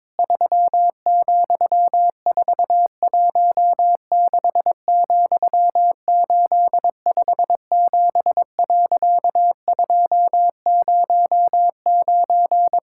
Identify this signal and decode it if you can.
Morse code, heard as a keyed tone, about 22 wpm, 700 Hz. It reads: 3,416,857.209
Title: Pangram Challenge v2 - 22wpm